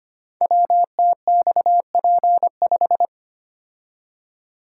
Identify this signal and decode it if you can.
Morse code, heard as a keyed tone, about 25 wpm, 700 Hz. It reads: WTXP5
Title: Code Groups - 5 Letter:Numbers - 25wpm